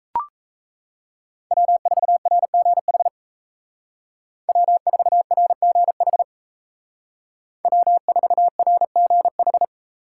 Morse code, heard as a keyed tone, about 35 wpm, 700 Hz. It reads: W4RGH W4RGH W4RGH
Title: Speed Racing - USA 1x3 Call Signs 28wpm